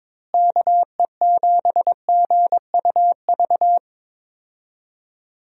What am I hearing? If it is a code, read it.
Morse code, heard as a keyed tone, about 22 wpm, 700 Hz. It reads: KE7GUV
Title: USA Call Signs - 6 Characters - Repeated 2x - 22wpm